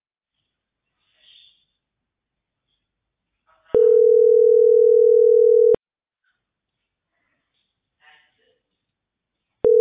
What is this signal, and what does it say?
Ringback tone (North America), 440 + 480 Hz.